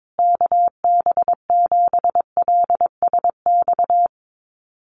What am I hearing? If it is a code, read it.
Morse code, heard as a keyed tone, about 22 wpm, 700 Hz. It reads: KB7LSX
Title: USA Call Signs Repeated 2x 22wpm